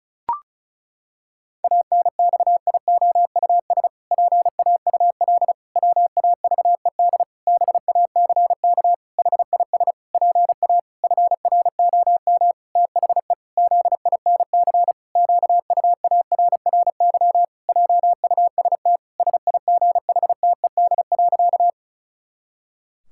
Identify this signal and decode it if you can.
Morse code, heard as a keyed tone, about 36 wpm, 700 Hz. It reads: ANXIOUS PAUL WAVED BACK HIS PA FROM THE ZINC QUARRY JUST SIGHTED.
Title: Pangram Challenge v2 - 35wpm